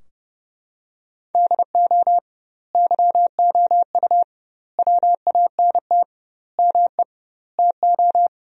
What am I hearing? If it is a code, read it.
Morse code, heard as a keyed tone, about 32 wpm, 700 Hz. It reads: DO YOU WANT ME TO
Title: 2x Word-Spacing - Sentences from Top 200 Words - 30wpm